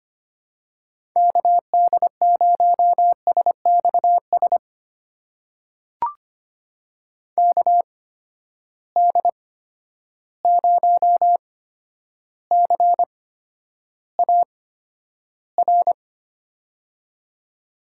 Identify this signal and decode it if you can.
Morse code, heard as a keyed tone, about 25 wpm, spaced as for 11 wpm, 700 Hz. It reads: KD0SXS KD0CAR
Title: USA Call Signs - 6 Characters - 8x Character Spacing - 25wpm